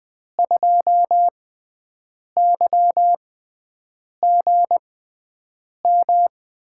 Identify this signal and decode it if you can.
Morse code, heard as a keyed tone, about 20 wpm, spaced as for 7 wpm, 700 Hz. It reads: 2YGM
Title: USA Call Signs - 6 Characters - 6x Character Spacing - 20wpm